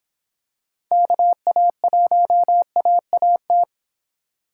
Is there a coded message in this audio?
Yes: Morse code from 0.9 to 3.6 s, heard as a keyed tone, about 26 wpm, 700 Hz. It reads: KA1AAT